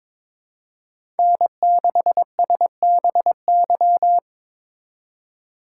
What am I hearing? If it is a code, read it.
Morse code, heard as a keyed tone, about 22 wpm, 700 Hz. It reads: N6SBY
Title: USA Call Signs - 5 Characters - 2x Character Spacing - 22wpm